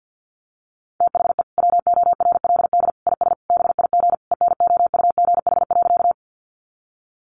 Morse code, heard as a keyed tone, about 49 wpm, 700 Hz. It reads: THE WORLD IS BIG ENOUGH.
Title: Sentences from Top 400 Words 50wpm